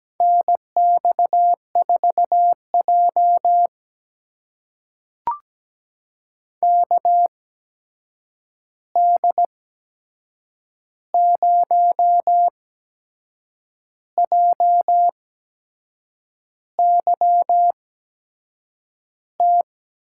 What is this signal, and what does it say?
Morse code, heard as a keyed tone, about 17 wpm, spaced as for 6 wpm, 700 Hz. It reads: NX4J KD0JYT
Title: USA Call Signs - 8x Character Spacing - 17wpm